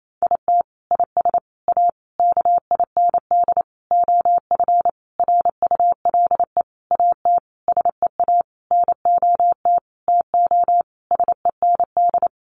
Morse code, heard as a keyed tone, about 28 wpm, 700 Hz. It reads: IT IS A KIND OF RULE AT SEA NOT TO SEND
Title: Sentences from Top 1500 Words 28wpm